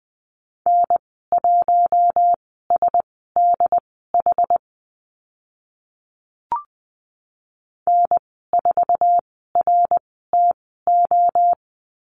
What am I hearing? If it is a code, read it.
Morse code, heard as a keyed tone, about 20 wpm, spaced as for 15 wpm, 700 Hz. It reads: N1SDH N4RTO
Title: 069 Lesson 12g - Letter d - Call Signs 20wpm